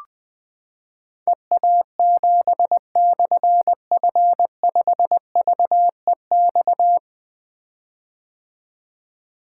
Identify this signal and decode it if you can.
Morse code, heard as a keyed tone, about 20 wpm, 700 Hz. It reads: EA7/F5VEX
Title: Difficult Call Signs 20wpm